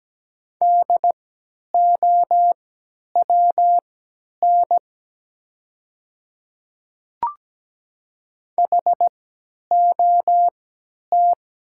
Morse code, heard as a keyed tone, about 17 wpm, spaced as for 10 wpm, 700 Hz. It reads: DOWN HOT
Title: 3x Character Spacing - Top 100 Words - 17wpm